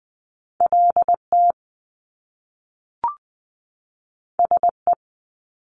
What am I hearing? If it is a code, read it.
Morse code, heard as a keyed tone, about 20 wpm, 700 Hz. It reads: LT SE
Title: Top Two Letter Combinations in Words 20wpm